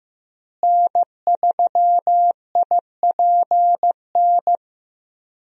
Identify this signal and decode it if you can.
Morse code, heard as a keyed tone, about 15 wpm, 700 Hz. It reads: N3IPN
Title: USA Call Signs - 5 Characters - Repeated 3x - 15wpm